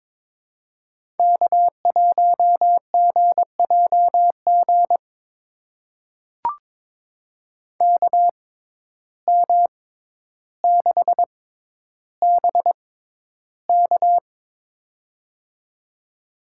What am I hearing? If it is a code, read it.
Morse code, heard as a keyed tone, about 22 wpm, spaced as for 11 wpm, 700 Hz. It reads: K1GJG KM6BK
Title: USA Call Signs - 5 Characters - 6x Character Spacing - 22wpm